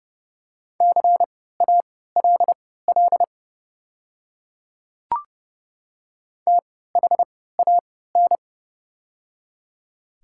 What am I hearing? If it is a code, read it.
Morse code, heard as a keyed tone, about 30 wpm, spaced as for 17 wpm, 700 Hz. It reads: CALL THAN
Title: 3x Character Spacing - Top 100 Words - 30wpm